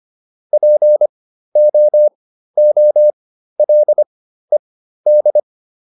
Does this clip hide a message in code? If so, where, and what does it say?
Yes: Morse code from 0.5 to 5.4 s, heard as a keyed tone, about 25 wpm, spaced as for 13 wpm, 600 Hz. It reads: POOLED